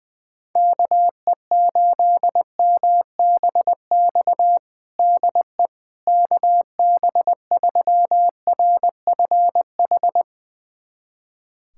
Morse code, heard as a keyed tone, about 20 wpm, 700 Hz. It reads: KE8MBX DE KB3RFH
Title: USA Callsign DE Callsign 20wpm